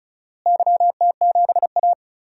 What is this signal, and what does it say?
Morse code, heard as a keyed tone, about 34 wpm, 700 Hz. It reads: YT7A
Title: DXCC-100 Call Signs - 4 Characters - Repeated 3x - 35wpm